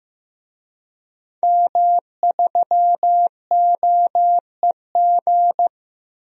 Morse code, heard as a keyed tone, about 15 wpm, 700 Hz. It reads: M3OEG
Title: DXCC-100 Call Signs - 5 Characters - 15wpm